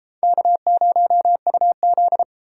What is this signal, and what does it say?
Morse code, heard as a keyed tone, about 32 wpm, 700 Hz. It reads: K0UZ